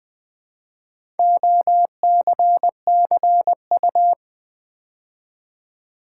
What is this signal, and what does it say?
Morse code, heard as a keyed tone, about 20 wpm, 700 Hz. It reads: OCCU
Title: Build A Word - 4 Letters At A Time - 10-Letter Words - 20wpm